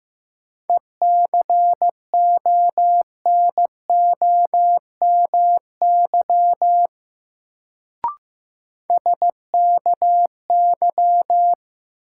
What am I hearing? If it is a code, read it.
Morse code, heard as a keyed tone, about 15 wpm, 700 Hz. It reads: ECONOMY SKY
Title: Sentences from Top 1200 Words - New Word Drill Alt - 15wpm